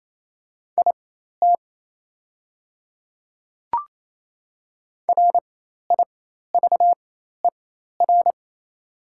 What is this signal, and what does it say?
Morse code, heard as a keyed tone, about 28 wpm, spaced as for 13 wpm, 700 Hz. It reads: IT RIVER
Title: 4x Character Spacing - Top 300 Words - 28wpm